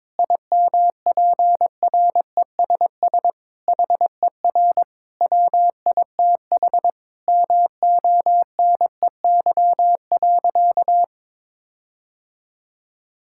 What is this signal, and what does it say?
Morse code, heard as a keyed tone, about 22 wpm, 700 Hz. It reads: IMPRESS HER WITH MONEY.
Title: Sentences from Top 1600 Words 22wpm